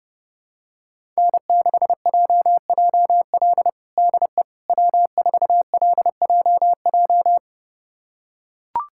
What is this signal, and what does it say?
Morse code, heard as a keyed tone, about 30 wpm, 700 Hz. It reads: N6JJL DE W4LJJ
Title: USA Callsign DE Callsign - 5 Characters - 30wpm